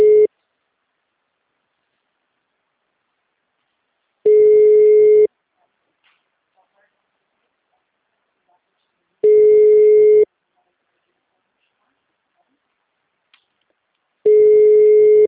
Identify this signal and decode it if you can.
Ringback tone (Europe, 425 Hz), with speech.
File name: out- -2011- - - .11029